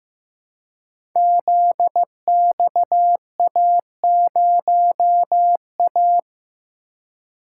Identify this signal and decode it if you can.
Morse code, heard as a keyed tone, about 15 wpm, 700 Hz. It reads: ZXA0A
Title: Code Groups - 5 Letter:Numbers - 15wpm